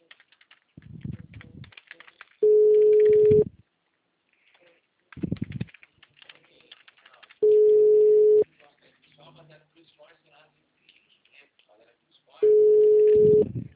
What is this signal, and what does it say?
Ringback tone (Europe, 425 Hz), with speech.
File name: out- -2046- - - .6009